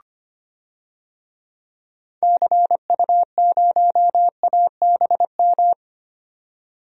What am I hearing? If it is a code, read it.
Morse code, heard as a keyed tone, about 25 wpm, 690 Hz. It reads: CU0ABM
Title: DXCC-100 Call Signs - 6 Characters - 25wpm